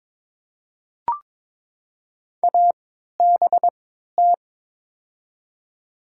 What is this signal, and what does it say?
Morse code, heard as a keyed tone, about 21 wpm, spaced as for 12 wpm, 700 Hz. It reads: ABT
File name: Top 90 QSO Elements - 3x Character Spacing - 22wpm